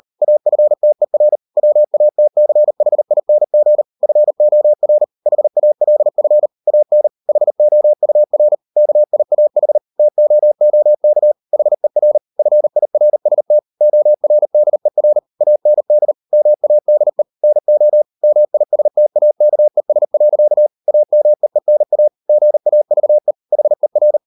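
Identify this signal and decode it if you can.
Morse code, heard as a keyed tone, about 38 wpm, 600 Hz. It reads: AFTER WATCHING FOR HALF AN HOUR KIAH TOOK HER FIRST ORDER AND MADE NO MISTAKES. AMEENA GAVE HER
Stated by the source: Morse code practice at 38 WPM, sending Ken Follett's novel Never in English, from CW med Gnister.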